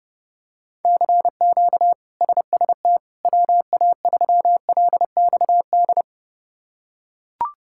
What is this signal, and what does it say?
Morse code, heard as a keyed tone, about 32 wpm, 700 Hz. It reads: CQ SST WA3LXD
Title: K1USN SST QSO Practice - 30wpm